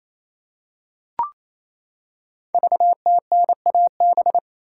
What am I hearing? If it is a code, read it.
Morse code, heard as a keyed tone, about 28 wpm, 700 Hz. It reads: VTNAB
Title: Code Groups - 5 Letters - 28wpm